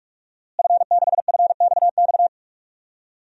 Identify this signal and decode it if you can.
Morse code, heard as a keyed tone, about 44 wpm, 700 Hz. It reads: F6FXX
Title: DXCC-100 Call Signs - 5 Characters - Repeated 3x - 45wpm